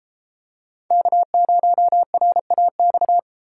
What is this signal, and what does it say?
Morse code, heard as a keyed tone, about 33 wpm, 700 Hz. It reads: K0RAX